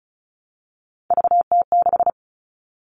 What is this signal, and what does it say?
Morse code, heard as a keyed tone, about 34 wpm, 700 Hz. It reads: VT6
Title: All Cut Numbers - 3-Digit 35wpm